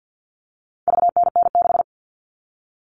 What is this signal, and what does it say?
Morse code, heard as a keyed tone, about 50 wpm, 700 Hz. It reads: VNN6
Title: All Cut Numbers - 4-Digit 50wpm